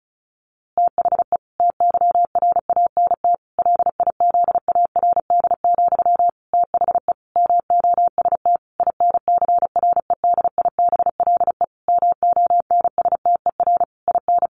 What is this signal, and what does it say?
Morse code, heard as a keyed tone, about 34 wpm, 700 Hz. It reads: THE TYRANT LIZARD, THE MOST INCREDIBLE MONSTER IN
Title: A Sound Of Thunder 35wpm